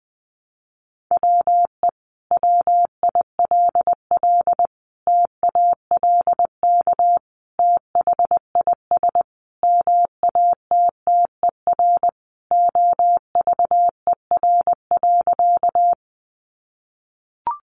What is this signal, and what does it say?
Morse code, heard as a keyed tone, about 20 wpm, 700 Hz. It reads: WE WILL TALK THIS MATTER OVER.
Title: Sentences from Top 700 Words 20wpm